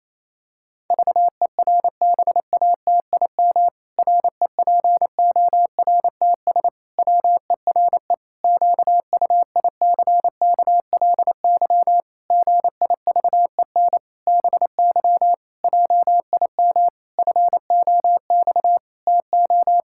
Morse code, heard as a keyed tone, about 27 wpm, 700 Hz. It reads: VERBATIM REPORTS WERE QUICKLY GIVEN BY JIM FOX TO
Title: Pangram Challenge v2 - 28wpm